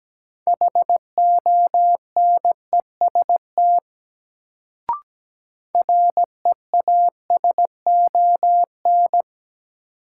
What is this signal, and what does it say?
Morse code, heard as a keyed tone, about 17 wpm, 700 Hz. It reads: HONEST REASON
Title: Sets of 5 Words Encore - New Word Drill - 17wpm